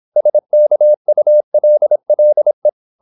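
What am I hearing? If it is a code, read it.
Morse code, heard as a keyed tone, about 26 wpm, 600 Hz. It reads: SKULLE
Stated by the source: High-speed (QRQ) Morse at 26 wpm.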